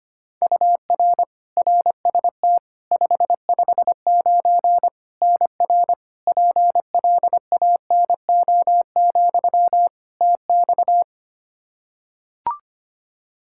Morse code, heard as a keyed tone, about 25 wpm, 700 Hz. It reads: UR RST 559 NR PLANO, TX
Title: Practice QSO - Exchange 1 - Beginning - 25wpm